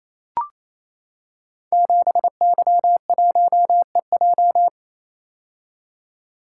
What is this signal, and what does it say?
Morse code, heard as a keyed tone, about 27 wpm, 700 Hz. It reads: 7Y1EJ
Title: Code Groups - 5 Letter:Numbers - 28wpm